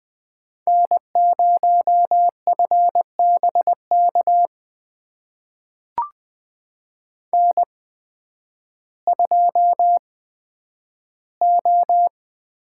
Morse code, heard as a keyed tone, about 20 wpm, spaced as for 11 wpm, 700 Hz. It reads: N0FBK N2O
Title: USA Call Signs - 5 Characters - 8x Character Spacing - 20wpm